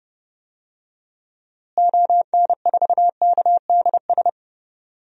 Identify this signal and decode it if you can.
Morse code, heard as a keyed tone, about 30 wpm, 700 Hz. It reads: ON4KDS
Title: DXCC-100 Call Signs - 6 Characters - Repeated 3x - 30wpm